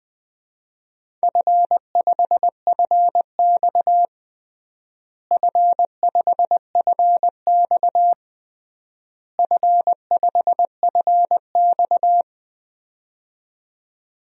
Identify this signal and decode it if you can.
Morse code, heard as a keyed tone, about 20 wpm, 700 Hz. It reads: F5FX F5FX F5FX
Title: DXCC-100 Call Signs - 4 Characters - Repeated 3x - 20wpm